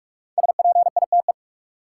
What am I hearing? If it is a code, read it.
Morse code, heard as a keyed tone, about 45 wpm, 700 Hz. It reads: SPITE
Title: Five Letter Words - 45wpm